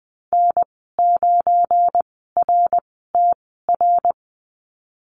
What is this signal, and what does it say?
Morse code, heard as a keyed tone, about 19 wpm, spaced as for 14 wpm, 700 Hz. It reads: N9RTR
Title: 173 Lesson 28g - Number 9 - Call Signs 20wpm